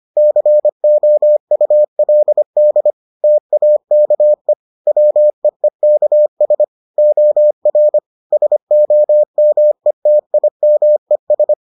Morse code, heard as a keyed tone, about 25 wpm, 600 Hz. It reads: COULD TAKE WEEKS OR SOMETIMES